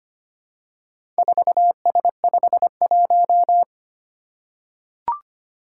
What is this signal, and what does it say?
Morse code, heard as a keyed tone, about 25 wpm, 700 Hz. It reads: 4S51
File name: practice / 25wpm / Four Letter-Number Combinations - 25wpm